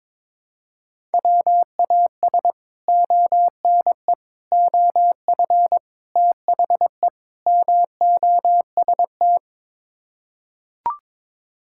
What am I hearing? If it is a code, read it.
Morse code, heard as a keyed tone, about 22 wpm, 700 Hz. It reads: WAS ONE OF THE MOST
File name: practice / 22wpm / Build A Phrase - 5-Word Phrases - 22wpm